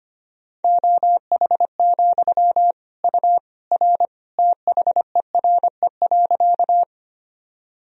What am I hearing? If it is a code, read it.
Morse code, heard as a keyed tone, about 25 wpm, 700 Hz. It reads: OH, U R THERE.
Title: Sentences from Top 500 Words and Abbreviations 25wpm